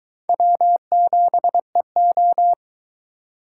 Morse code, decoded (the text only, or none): W7EO